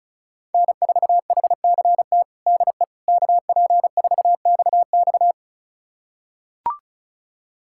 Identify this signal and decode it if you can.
Morse code, heard as a keyed tone, about 34 wpm, 700 Hz. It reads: N4HCT DE KP4XX
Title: USA Callsign DE Callsign - 5 Characters - 35wpm